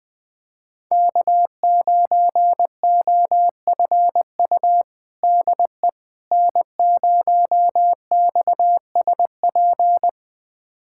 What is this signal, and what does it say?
Morse code, heard as a keyed tone, about 21 wpm, 700 Hz. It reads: K9OFU DE N0XSP
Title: USA Callsign DE Callsign - 5 Characters - 20wpm